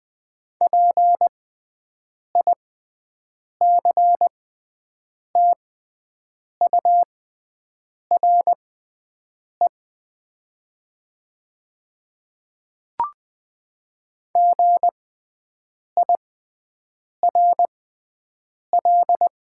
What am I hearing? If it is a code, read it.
Morse code, heard as a keyed tone, about 20 wpm, spaced as for 7 wpm, 700 Hz. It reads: PICTURE GIRL
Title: 6x Character Spacing - Top 500 Words - 20wpm